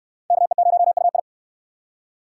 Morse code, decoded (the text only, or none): KE0UI